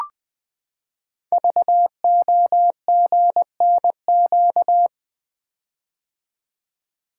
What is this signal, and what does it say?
Morse code, heard as a keyed tone, about 20 wpm, 700 Hz. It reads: VOGNQ
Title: Code Groups - 5 Letters - 20wpm